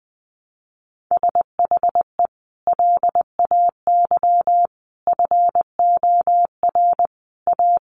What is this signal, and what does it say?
Morse code, heard as a keyed tone, about 19 wpm, 700 Hz. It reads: SHE LAY FOR A
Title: Sentences from Top 900 Words 20wpm